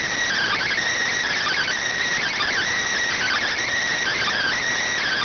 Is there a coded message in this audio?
Yes: Emergency Alert System (SAME) tones (header bursts) from 0.3 to 4.6 s, heard as harsh buzzing bursts.